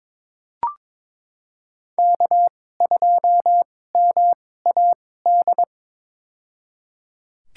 Morse code, heard as a keyed tone, about 22 wpm, spaced as for 16 wpm, 700 Hz. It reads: K2MAD
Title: USA Call Signs - 2x Character Spacing - 22wpm